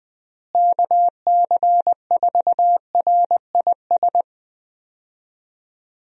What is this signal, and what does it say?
Morse code, heard as a keyed tone, about 21 wpm, 700 Hz. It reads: KC4RIS
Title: USA Call Signs - 6 Characters - Repeated 3x - 20wpm